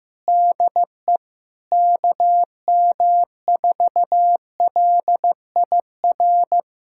Morse code, heard as a keyed tone, about 15 wpm, 700 Hz. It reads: DE KM4LIR
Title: USA Callsign DE Callsign 15wpm